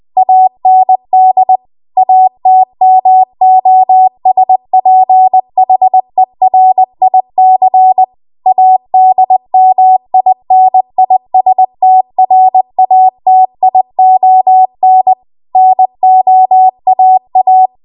Morse code, heard as a keyed tone, about 20 wpm, 750 Hz. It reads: AND ATMOSPHERIC ADMINISTRATION NOAA